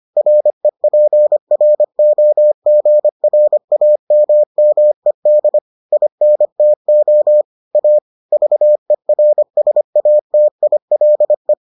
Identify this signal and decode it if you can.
Morse code, heard as a keyed tone, about 25 wpm, 600 Hz. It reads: REPROGRAMMED INTO A VERSATILE